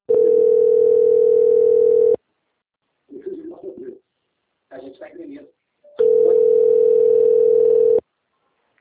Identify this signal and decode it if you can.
Ringback tone (North America), 440 + 480 Hz, with speech.
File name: out- - - - - .16237